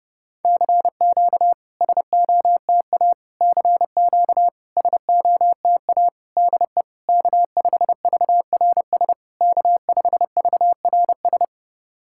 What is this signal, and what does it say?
Morse code, heard as a keyed tone, about 30 wpm, 690 Hz. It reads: CQ SOTA CQ SOTA DE K5VRS K5VRS
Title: SOTA QSO Practice 30wpm